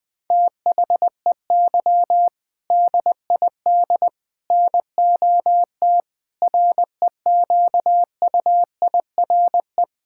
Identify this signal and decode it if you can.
Morse code, heard as a keyed tone, about 20 wpm, 700 Hz. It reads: THEY DID NOT REQUIRE
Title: Sentences from Top 1600 Words 20wpm